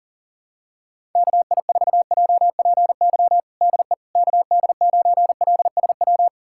Morse code, heard as a keyed tone, about 40 wpm, 700 Hz. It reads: KI4JPY DE KD9LSW
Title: USA Callsign DE Callsign 40wpm